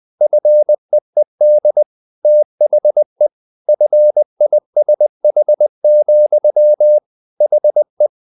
Morse code, heard as a keyed tone, about 20 wpm, 600 Hz. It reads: FEED THE FISH, HE
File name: the old man and the sea / the old man and the sea part 1 20wpm